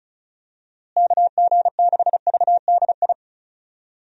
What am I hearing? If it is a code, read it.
Morse code, heard as a keyed tone, about 36 wpm, 700 Hz. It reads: KG6VDI
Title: USA Call Signs - 6 Characters - Repeated 3x - 35wpm